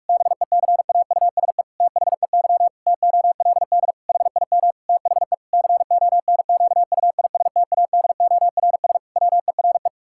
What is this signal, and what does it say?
Morse code, heard as a keyed tone, about 41 wpm, 700 Hz. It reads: BECAUSE THEY TOLD HIM THE CONQUISTADORS WERE
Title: Pangram Challenge v2 - 45wpm